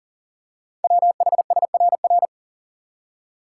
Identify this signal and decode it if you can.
Morse code, heard as a keyed tone, about 40 wpm, 700 Hz. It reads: WHSRR
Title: Code Groups - 5 Letters - 40wpm